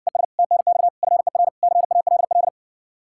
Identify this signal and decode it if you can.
Morse code, heard as a keyed tone, about 60 wpm, 700 Hz. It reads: ES TNX FER CALL